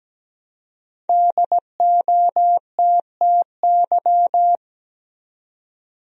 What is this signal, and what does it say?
Morse code, heard as a keyed tone, about 17 wpm, 700 Hz. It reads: DOTTY
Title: USA Common Names - Max 5 Letters - 17wpm